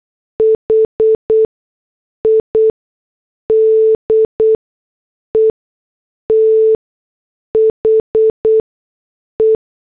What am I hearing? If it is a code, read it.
Morse code, heard as a keyed tone, about 8 wpm, 440 Hz. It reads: H I D E T H E
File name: morse / headlines